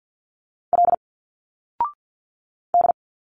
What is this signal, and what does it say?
Morse code, heard as a keyed tone, about 51 wpm, 700 Hz. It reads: L D
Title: Single Letters 50wpm